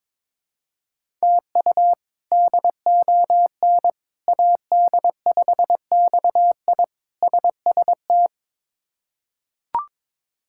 Morse code, heard as a keyed tone, about 22 wpm, 700 Hz. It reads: TU DON AD5XI SST
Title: K1USN SST QSO Practice - 22wpm